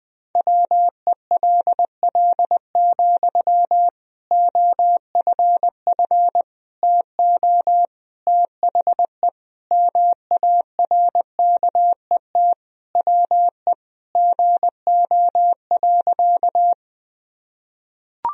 Morse code, heard as a keyed tone, about 20 wpm, 700 Hz. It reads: WELL, OFF TO THE MARKET WE GO.
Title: Sentences from Top 1000 Words - Review Part 3 - 20wpm